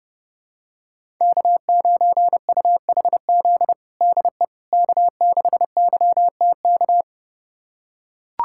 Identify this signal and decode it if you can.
Morse code, heard as a keyed tone, about 32 wpm, 700 Hz. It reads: K9UHZ DE K6YTK
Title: USA Callsign DE Callsign - 5 Characters - 30wpm